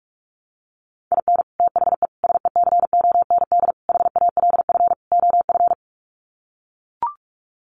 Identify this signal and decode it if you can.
Morse code, heard as a keyed tone, about 44 wpm, 700 Hz. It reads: IN THE SECOND HALF OF
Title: Sets of 5 Words 45wpm